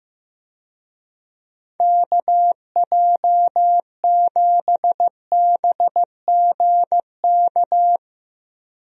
Morse code, heard as a keyed tone, about 16 wpm, 700 Hz. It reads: KJ7BGK
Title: USA Call Signs - 6 Characters - 15wpm